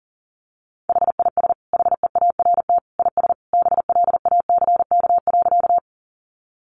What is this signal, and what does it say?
Morse code, heard as a keyed tone, about 40 wpm, 700 Hz. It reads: HIS HEART IS BLACK.
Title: Sentences from Top 1200 Words 40wpm